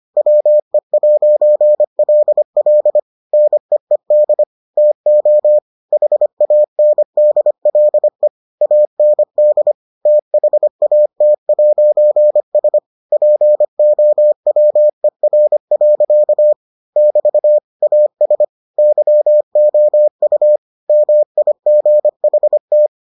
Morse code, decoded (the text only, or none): WE'LL NEED TO HANDLE AND THAT'S POWER. <BT> AS YOU MIGHT